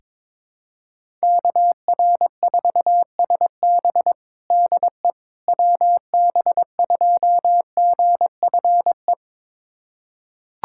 Morse code, heard as a keyed tone, about 22 wpm, 700 Hz. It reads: KR4SB DE WB2GFE
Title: USA Callsign DE Callsign 22wpm